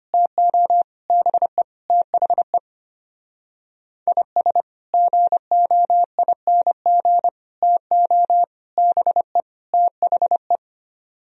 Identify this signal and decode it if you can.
Morse code, heard as a keyed tone, about 33 wpm, 700 Hz. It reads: TO BE THE IS GOING TO BE THE